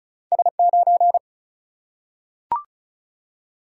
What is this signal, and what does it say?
Morse code, heard as a keyed tone, about 36 wpm, 700 Hz. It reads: S9
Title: Single-Letter Single-Number 35wpm